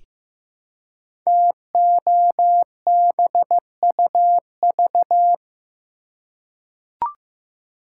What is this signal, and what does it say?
Morse code, heard as a keyed tone, about 15 wpm, 700 Hz. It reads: TOBUV
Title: Code Groups - 5 Letters - 15wpm